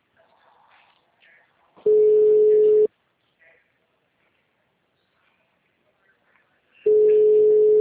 Ringback tone (Europe, 425 Hz).
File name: out- -2011- - - .64